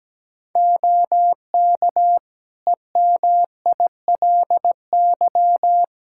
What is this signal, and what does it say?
Morse code, heard as a keyed tone, about 17 wpm, 700 Hz. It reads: OK EMILY
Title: Practice QSO - Exchange 3 - More Info - 17wpm